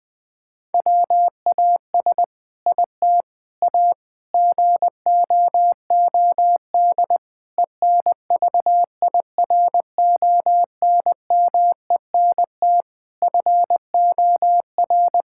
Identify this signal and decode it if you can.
Morse code, heard as a keyed tone, about 20 wpm, 700 Hz. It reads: WAS IT A GOOD ENVIRONMENT FOR
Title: Sentences from Top 1000 Words - Review Part 5 - 20wpm